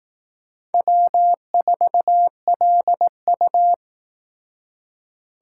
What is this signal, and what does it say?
Morse code, heard as a keyed tone, about 19 wpm, 700 Hz. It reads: W4LU